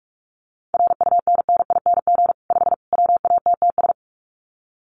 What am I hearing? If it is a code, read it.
Morse code, heard as a keyed tone, about 44 wpm, 700 Hz. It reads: RUNNING 5 WATTS
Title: Common QSO Phrases 45wpm